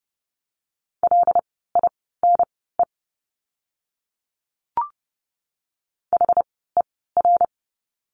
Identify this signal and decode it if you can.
Morse code, heard as a keyed tone, about 30 wpm, spaced as for 17 wpm, 700 Hz. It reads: LINE HER
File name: course / 30:10wpm / 115 Lesson 19f - Letter w - Words 30wpm